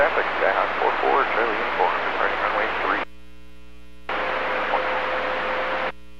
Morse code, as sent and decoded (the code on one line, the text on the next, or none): none